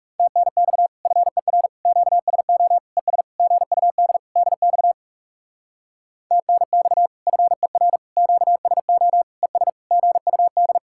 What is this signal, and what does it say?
Morse code, heard as a keyed tone, about 37 wpm, 700 Hz. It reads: TNX FER QSO ES GUD DX TNX FER QSO ES GUD
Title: Speed-Racing Common QSO Phrases - 30wpm